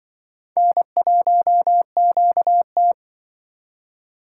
Morse code, heard as a keyed tone, about 23 wpm, 700 Hz. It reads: N1QT
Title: Speed Racing - USA 1x2 Call Signs 20wpm